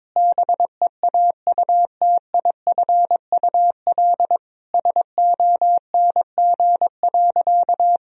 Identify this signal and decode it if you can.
Morse code, heard as a keyed tone, about 22 wpm, 700 Hz. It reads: BEAUTIFUL SONG.
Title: Sentences from Top 1000 Words - Review Part 2 - 22wpm